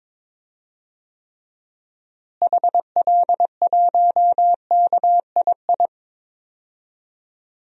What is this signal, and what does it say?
Morse code, heard as a keyed tone, about 22 wpm, 700 Hz. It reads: HL1KII
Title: DXCC-100 Call Signs - 6 Characters - Repeated 2x - 22wpm